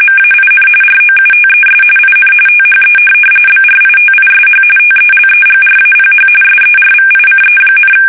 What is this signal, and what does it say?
RTTY, about 45.45 baud. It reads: MAIN ROUTES IN THE BRITISH TELEGRAPH SERVICE,